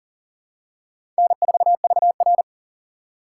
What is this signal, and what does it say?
Morse code, heard as a keyed tone, about 37 wpm, 700 Hz. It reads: N4VR
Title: USA Call Signs - 6x Character Spacing - 40wpm